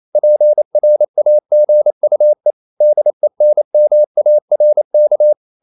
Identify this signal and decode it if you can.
Morse code, heard as a keyed tone, about 28 wpm, 600 Hz. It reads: PRAGUE DENMARK
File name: CW / Practice / NationalCapitals28